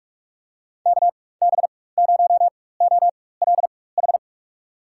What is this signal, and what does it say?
Morse code, heard as a keyed tone, about 43 wpm, spaced as for 21 wpm, 700 Hz. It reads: KB0OLH
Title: USA Call Signs - 4x Character Spacing - 45wpm